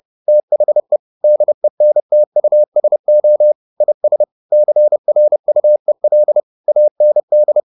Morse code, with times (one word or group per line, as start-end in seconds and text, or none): 0.3-1.0 THE
1.2-3.5 DENTUSO
3.8-4.2 IS
4.5-6.4 CRUEL
6.7-7.6 AND